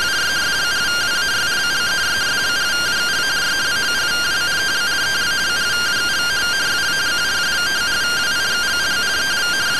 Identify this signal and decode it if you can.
Two-tone teleprinter signal, about 170 Hz shift.